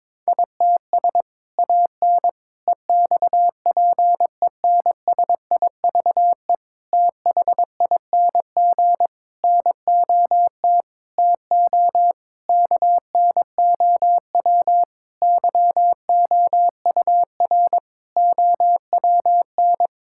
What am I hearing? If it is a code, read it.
Morse code, heard as a keyed tone, about 22 wpm, 700 Hz. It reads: ITS AN EXPENSIVE THING NOT TO KNOW YOUR OWN
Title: Sentences from Top 1000 Words - Review Part 6 - 22wpm